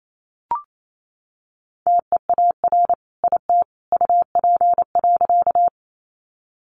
Morse code, heard as a keyed tone, about 28 wpm, 700 Hz. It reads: TEAR IT UP.
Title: A Sound Of Thunder 28wpm